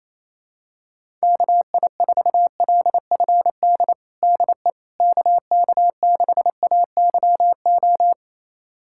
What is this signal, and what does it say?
Morse code, heard as a keyed tone, about 28 wpm, 700 Hz. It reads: KI4LFD DE KK6AYO
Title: USA Callsign DE Callsign 28wpm